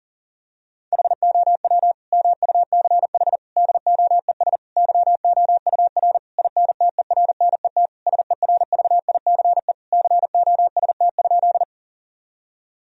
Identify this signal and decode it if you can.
Morse code, heard as a keyed tone, about 37 wpm, 700 Hz. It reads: HOW MUCH DOES YOUR INTERNET SERVICE COST?
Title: Sentences from Top 1000 Words - Review Part 4 - 40wpm